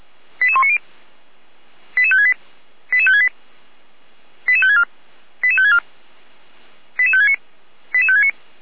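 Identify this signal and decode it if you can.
Five-tone selective call (ZVEI), heard as quick runs of five tones, tones of about 70 ms. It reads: 88190 88568 88568 88564 88564 88569 88569
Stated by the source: Received on an SDR as a ZVEI five-tone call.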